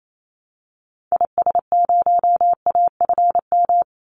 Morse code, heard as a keyed tone, about 28 wpm, 700 Hz. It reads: IS0AFM
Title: Call Signs - 28wpm